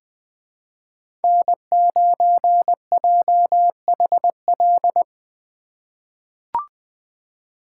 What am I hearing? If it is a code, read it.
Morse code, heard as a keyed tone, about 21 wpm, 700 Hz. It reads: N9JHL
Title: K1USN SST QSO Practice - 20wpm